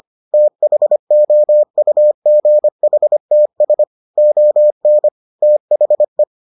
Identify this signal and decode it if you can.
Morse code, heard as a keyed tone, about 25 wpm, 600 Hz. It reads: THOUGHTS ON THE